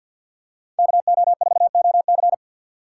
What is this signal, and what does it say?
Morse code, heard as a keyed tone, about 50 wpm, 700 Hz. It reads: KO4OC
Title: USA Call Signs - 5 Characters - 4x Character Spacing - 50wpm